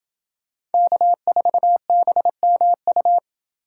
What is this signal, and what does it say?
Morse code, heard as a keyed tone, about 25 wpm, 700 Hz. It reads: K4BMU
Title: Speed Racing - USA 1x3 Call Signs 20wpm